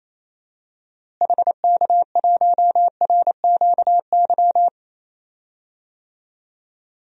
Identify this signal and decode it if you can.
Morse code, heard as a keyed tone, about 28 wpm, 700 Hz. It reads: HK1RQY
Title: DXCC-100 Call Signs - 6 Characters - Repeated 2x - 28wpm